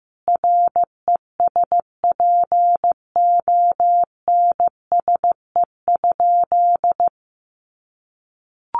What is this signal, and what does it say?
Morse code, heard as a keyed tone, about 15 wpm, 700 Hz. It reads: RESPONSE?
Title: Sentences from Top 1400 Words 15wpm